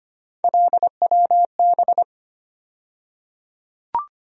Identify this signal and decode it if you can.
Morse code, heard as a keyed tone, about 25 wpm, 700 Hz. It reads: LWB
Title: USA Call Sign Postfixes - 3 Letters 25wpm